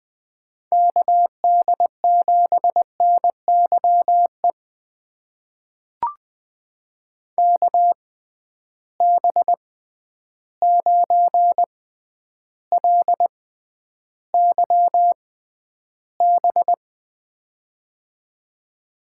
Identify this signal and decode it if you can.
Morse code, heard as a keyed tone, about 20 wpm, spaced as for 10 wpm, 700 Hz. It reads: KD7NYE KB9LYB
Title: USA Call Signs - 6 Characters - 6x Character Spacing - 20wpm